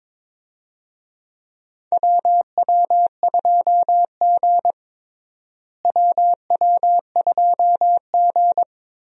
Morse code, heard as a keyed tone, about 21 wpm, 700 Hz. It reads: WW2G WW2G
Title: USA Call Signs - 4 Characters - Repeated 2x - 22wpm